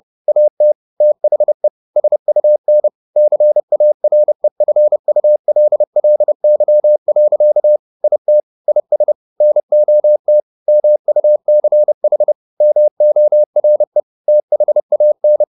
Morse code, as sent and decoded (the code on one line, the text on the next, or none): .- - / - .... / ... ..- -. / -.-. .- .-. ..-. ..- .-.. .-.. -.-- .-.-.- / .. - / .. ... / -. --- - / -- ..- -.-. .... / -- --- .-. / - .... .- -.
AT THE SUN CAREFULLY. IT IS NOT MUCH MORE THAN